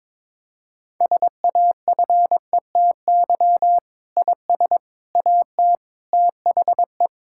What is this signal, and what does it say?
Morse code, heard as a keyed tone, about 22 wpm, 700 Hz. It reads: SAFETY IS AT THE
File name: Sentences with 7-Word Phrases - 22wpm